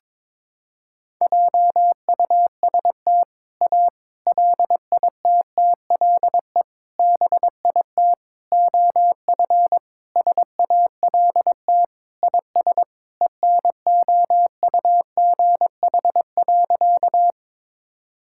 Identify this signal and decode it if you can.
Morse code, heard as a keyed tone, about 22 wpm, 700 Hz. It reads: JUST A LITTLE BIT OF SALT IS ENOUGH.
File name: practice / 22wpm / Sentences with 4-Word Phrases - 22wpm